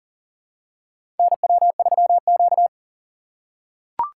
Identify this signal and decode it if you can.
Morse code, heard as a keyed tone, about 40 wpm, 700 Hz. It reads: NW3Q
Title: USA Call Signs - 4 Characters - 6x Character Spacing - 40wpm